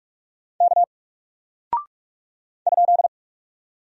Morse code, heard as a keyed tone, about 45 wpm, 700 Hz. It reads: K ?
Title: Single Character (Letter:Number:Punctuation:Prosign) Alt 45wpm